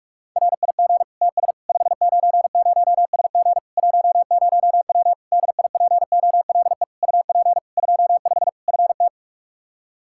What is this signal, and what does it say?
Morse code, heard as a keyed tone, about 46 wpm, 700 Hz. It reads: RIG TS 590SG 10W DIPOLE UP 25 FT
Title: K1USN SST QSO Practice 45wpm